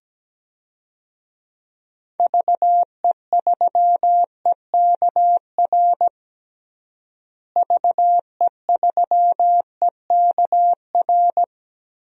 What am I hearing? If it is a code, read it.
Morse code, heard as a keyed tone, about 17 wpm, 700 Hz. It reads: VE3EKR VE3EKR
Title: DXCC-100 Call Signs - 6 Characters - Repeated 3x - 17wpm